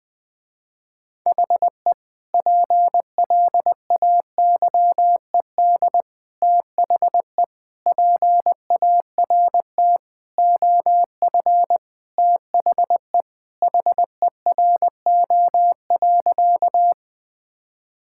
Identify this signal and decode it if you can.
Morse code, heard as a keyed tone, about 20 wpm, 700 Hz. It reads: HE PLAYED THE PART OF THE HERO.
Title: Sentences with 4-Word Phrases - Spoken Beginning - 20wpm